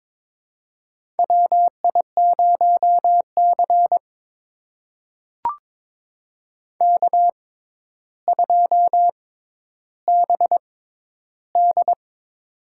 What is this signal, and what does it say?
Morse code, heard as a keyed tone, about 22 wpm, spaced as for 11 wpm, 700 Hz. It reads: WI0C K2BD
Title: USA Call Signs - 4 Characters - 6x Character Spacing - 22wpm